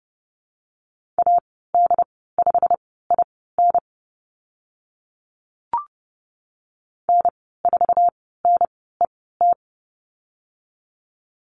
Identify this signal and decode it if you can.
Morse code, heard as a keyed tone, about 30 wpm, spaced as for 17 wpm, 700 Hz. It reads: AD5IN N4NET
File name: Lesson 15g - Number 5 - Call Signs 30wpm